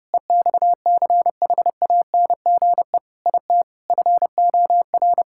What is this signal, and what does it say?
Morse code, heard as a keyed tone, about 32 wpm, 700 Hz. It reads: EXCHANGE IT FOR
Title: Sentences from Top 1000 Words - Review Part 6 - 30wpm